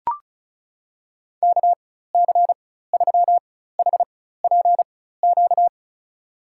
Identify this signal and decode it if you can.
Morse code, heard as a keyed tone, about 36 wpm, spaced as for 16 wpm, 700 Hz. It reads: KC3HPQ
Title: USA Call Signs - 6 Characters - 4x Character Spacing - 35wpm